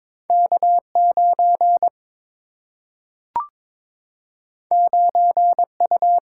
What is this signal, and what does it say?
Morse code, heard as a keyed tone, about 22 wpm, 700 Hz. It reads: K 9 9 U
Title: Single-Letter Single-Number 22wpm